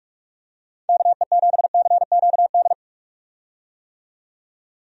Morse code, heard as a keyed tone, about 45 wpm, 700 Hz. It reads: KE7CQD
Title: USA Call Signs - 6 Characters - 45wpm